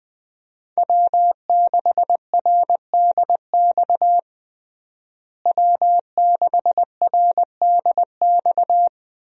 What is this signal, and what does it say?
Morse code, heard as a keyed tone, about 20 wpm, 700 Hz. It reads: W6RDX W6RDX
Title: USA Call Signs - 5 Characters - Repeated 2x - 20wpm